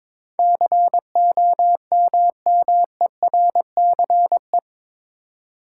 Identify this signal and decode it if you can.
Morse code, heard as a keyed tone, about 22 wpm, 700 Hz. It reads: COMMERCE
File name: practice / 22wpm / Eight Letter Words - 22wpm